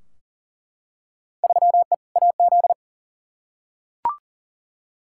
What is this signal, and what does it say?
Morse code, heard as a keyed tone, about 40 wpm, 700 Hz. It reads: 3E AZ
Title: Field Day Exchanges Encore - 40wpm